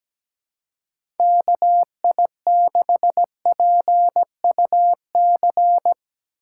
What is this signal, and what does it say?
Morse code, heard as a keyed tone, about 17 wpm, 700 Hz. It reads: KI6PUC
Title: USA Call Signs - 6 Characters - 3x Character Spacing - 17wpm